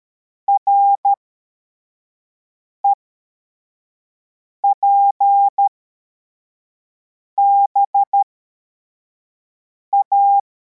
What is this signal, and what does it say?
Morse code, heard as a keyed tone, about 13 wpm, spaced as for 4 wpm, 800 Hz. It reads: REPBA
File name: formation / cw / Text81